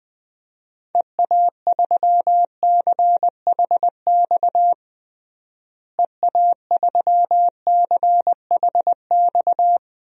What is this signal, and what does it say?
Morse code, heard as a keyed tone, about 20 wpm, 700 Hz. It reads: EA3CHX EA3CHX
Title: DXCC-100 Call Signs - 6 Characters - Repeated 2x - 20wpm